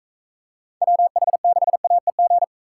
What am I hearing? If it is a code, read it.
Morse code, heard as a keyed tone, about 45 wpm, 700 Hz. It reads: WH6AEG